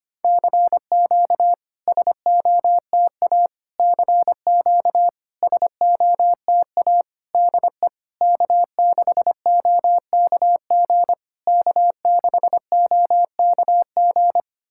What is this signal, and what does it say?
Morse code, heard as a keyed tone, about 25 wpm, 700 Hz. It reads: CQ SOTA CQ SOTA DE K6OKG K6OKG
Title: SOTA QSO Practice 25wpm